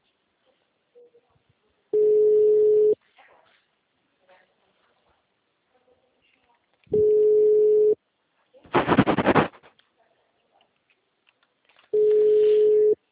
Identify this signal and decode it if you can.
Ringback tone (Europe, 425 Hz).